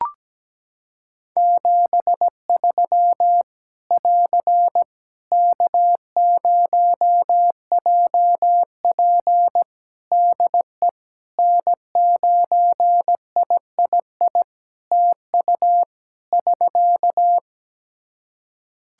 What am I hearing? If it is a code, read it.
Morse code, heard as a keyed tone, about 17 wpm, 700 Hz. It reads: 73 <AR> K0JP DE N9III TU <SK>
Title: Practice QSO - Exchange 4 - Ending - 17wpm